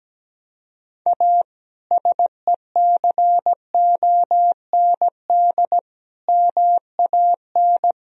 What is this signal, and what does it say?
Morse code, heard as a keyed tone, about 17 wpm, 700 Hz. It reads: A SECOND MAN
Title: Sentences from Top 1000 Words - Review Part 1 - 17wpm